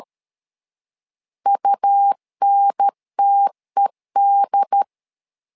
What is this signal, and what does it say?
Morse code, heard as a keyed tone, about 13 wpm, 800 Hz. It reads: UNTED
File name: LEZIONI / 3 MWRO / 3 MW 12wpm 1633